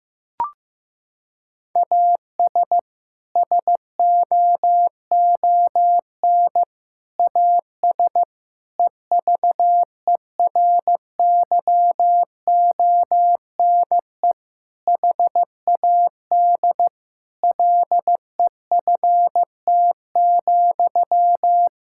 Morse code, heard as a keyed tone, about 15 wpm, 700 Hz. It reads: AS SOON AS EVERYONE HAD LEFT,
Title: Sentences from Top 1000 Words - Review Part 5 - 15wpm